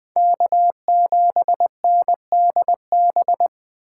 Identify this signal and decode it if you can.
Morse code, heard as a keyed tone, about 20 wpm, 700 Hz. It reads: K7NDB
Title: USA Call Signs - 5 Characters - 3x Character Spacing - 20wpm